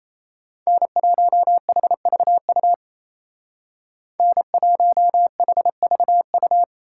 Morse code, heard as a keyed tone, about 34 wpm, 700 Hz. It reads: N1HVU N1HVU
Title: Speed Racing - USA 1x3 Call Signs 28wpm